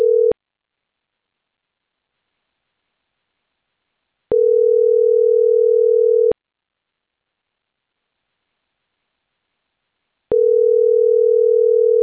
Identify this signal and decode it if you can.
Ringback tone (North America), 440 + 480 Hz.